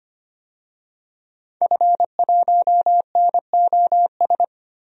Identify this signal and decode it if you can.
Morse code, heard as a keyed tone, about 25 wpm, 700 Hz. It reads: F1NOS
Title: DXCC-100 Call Signs - 5 Characters - 25wpm